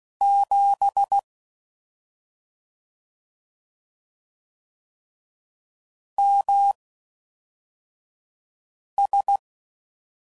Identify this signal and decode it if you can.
Morse code, heard as a keyed tone, about 16 wpm, spaced as for 4 wpm, 800 Hz. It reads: 7 MS
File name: Mixed / CW Comparisons / CW Comparisons Broken Characters